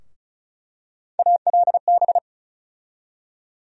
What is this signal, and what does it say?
Morse code, heard as a keyed tone, about 36 wpm, 700 Hz. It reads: ALB
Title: USA Call Sign Postfixes - 3 Letters 35wpm